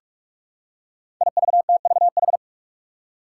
Morse code, heard as a keyed tone, about 46 wpm, 700 Hz. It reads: IVTVH
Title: Five Challenging Letter-Number Combinations 45wpm